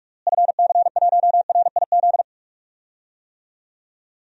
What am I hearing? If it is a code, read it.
Morse code, heard as a keyed tone, about 44 wpm, 700 Hz. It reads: FC1RIZ
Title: DXCC-100 Call Signs - 6 Characters - Repeated 2x - 45wpm